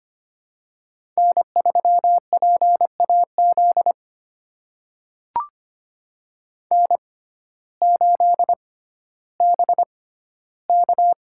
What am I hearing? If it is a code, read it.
Morse code, heard as a keyed tone, about 25 wpm, spaced as for 14 wpm, 700 Hz. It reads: N3PAZ N8BK
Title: USA Call Signs - 5 Characters - 6x Character Spacing - 25wpm